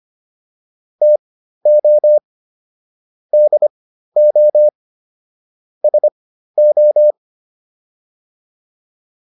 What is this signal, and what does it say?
Morse code, heard as a keyed tone, about 25 wpm, spaced as for 13 wpm, 600 Hz. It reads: TO DO SO